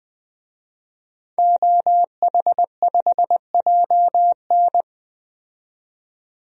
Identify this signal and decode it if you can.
Morse code, heard as a keyed tone, about 20 wpm, 700 Hz. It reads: OH5JN
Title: DXCC-100 Call Signs - 5 Characters - 20wpm